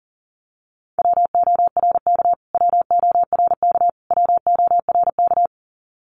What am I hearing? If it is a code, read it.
Morse code, heard as a keyed tone, about 40 wpm, 700 Hz. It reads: WORK WORK WORK
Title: Top 200 Words Repeated 40wpm